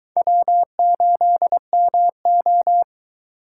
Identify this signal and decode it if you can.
Morse code, heard as a keyed tone, about 22 wpm, 700 Hz. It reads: W8MO